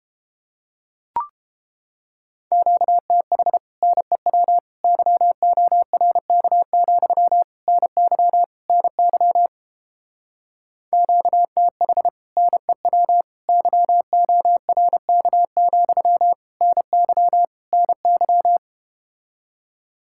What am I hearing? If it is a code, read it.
Morse code, heard as a keyed tone, about 29 wpm, 700 Hz. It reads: QTH NEW YORK, NY NY QTH NEW YORK, NY NY